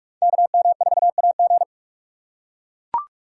Morse code, heard as a keyed tone, about 43 wpm, 700 Hz. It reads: KM4AG
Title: USA Call Signs - 4x Character Spacing - 45wpm